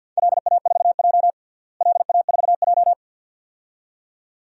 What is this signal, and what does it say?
Morse code, heard as a keyed tone, about 53 wpm, 700 Hz. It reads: LA4J LA4J
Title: DXCC-100 Call Signs - 4 Characters - Repeated 2x - 50wpm